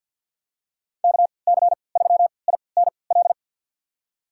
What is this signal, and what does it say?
Morse code, heard as a keyed tone, about 50 wpm, spaced as for 28 wpm, 700 Hz. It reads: KC3INL
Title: USA Call Signs - 6 Characters - 3x Character Spacing - 50wpm